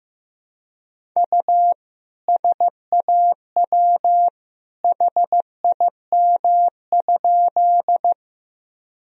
Morse code, heard as a keyed tone, about 15 wpm, 700 Hz. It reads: U SAW HIM?
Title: Sentences from Top 300 Words and Abbreviations 15wpm